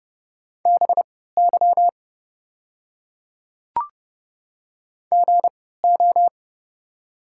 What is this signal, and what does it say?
Morse code, heard as a keyed tone, about 30 wpm, spaced as for 17 wpm, 690 Hz. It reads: BY GO
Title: 3x Character Spacing - Top 200 Words - 30wpm